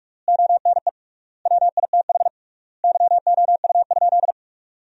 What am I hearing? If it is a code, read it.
Morse code, heard as a keyed tone, about 46 wpm, 700 Hz. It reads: ONE WITH YOU?
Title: 3x Word-Spacing - Sentences from Top 500 Words - 45wpm